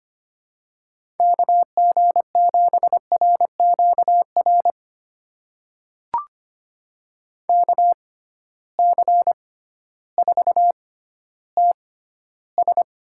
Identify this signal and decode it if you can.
Morse code, heard as a keyed tone, about 25 wpm, spaced as for 13 wpm, 700 Hz. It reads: KG7RQR KC4TS
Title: USA Call Signs - 6 Characters - 6x Character Spacing - 25wpm